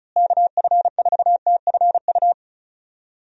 Morse code, heard as a keyed tone, about 36 wpm, 700 Hz. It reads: KF4TFU
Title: USA Call Signs - 6 Characters - Repeated 3x - 35wpm